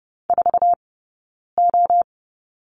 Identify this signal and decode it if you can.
Morse code, heard as a keyed tone, about 30 wpm, spaced as for 9 wpm, 690 Hz. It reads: 4O
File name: Lesson 17d - Letter u - Two Characters 30wpm